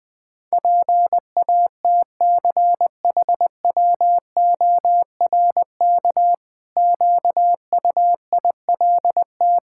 Morse code, heard as a keyed tone, about 21 wpm, 700 Hz. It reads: PATCHWORK QUILT
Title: Pangram Challenge v2 - 20wpm